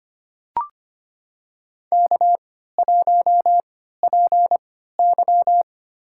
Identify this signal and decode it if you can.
Morse code, heard as a keyed tone, about 25 wpm, spaced as for 14 wpm, 700 Hz. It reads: K1PY
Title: USA Call Signs - 4 Characters - 3x Character Spacing - 25wpm